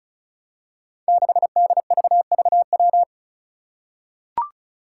Morse code, heard as a keyed tone, about 36 wpm, 700 Hz. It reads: 6DVVW
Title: Five Challenging Letter-Number Combinations 35wpm